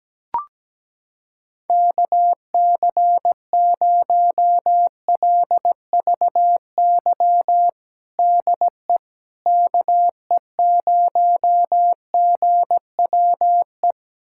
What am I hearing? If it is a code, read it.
Morse code, heard as a keyed tone, about 17 wpm, 700 Hz. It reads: KC0LVY DE KE0GWE
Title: USA Callsign DE Callsign 17wpm